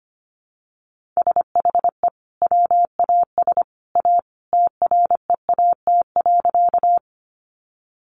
Morse code, heard as a keyed tone, about 25 wpm, 700 Hz. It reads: SHE WAS A TREAT.
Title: Sentences from Top 1300 Words 25wpm